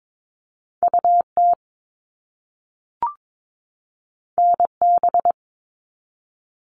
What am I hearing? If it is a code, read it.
Morse code, heard as a keyed tone, about 22 wpm, 700 Hz. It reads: UT NB
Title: All Cut Numbers - 2-Digit 22wpm